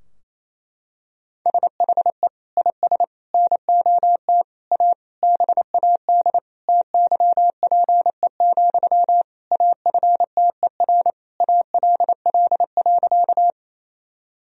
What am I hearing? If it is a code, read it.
Morse code, heard as a keyed tone, about 28 wpm, 700 Hz. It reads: SHE IS NOT A BAD TYPE, AFTER ALL.
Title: Sentences from Top 1000 Words - Review Part 3 - 28wpm